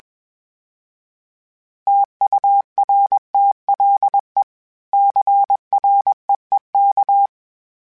Morse code, heard as a keyed tone, about 21 wpm, 800 Hz. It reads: TURTLE CREEK
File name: fromSMs / FromSMs mp3s / Cities@20